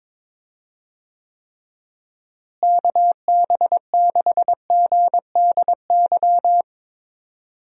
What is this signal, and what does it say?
Morse code, heard as a keyed tone, about 22 wpm, 700 Hz. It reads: KB6GDY
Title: USA Call Signs - 6 Characters - Repeated 3x - 22wpm